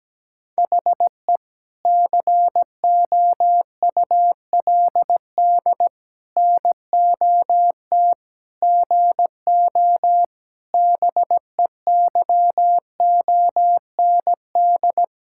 Morse code, heard as a keyed tone, about 17 wpm, 700 Hz. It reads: HE COULD NOT GO BEYOND
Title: Sentences from Top 1000 Words - Review Part 6 - 17wpm